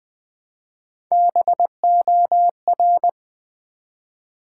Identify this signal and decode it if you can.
Morse code, heard as a keyed tone, about 20 wpm, 700 Hz. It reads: BOR
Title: Build A Word - 1 Letter At A Time - 3 to 5-Letter Words - 20wpm